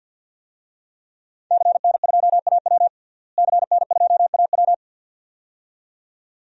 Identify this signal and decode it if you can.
Morse code, heard as a keyed tone, about 50 wpm, 700 Hz. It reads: CN2AW CN2AW
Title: DXCC-100 Call Signs - 5 Characters - Repeated 2x - 50wpm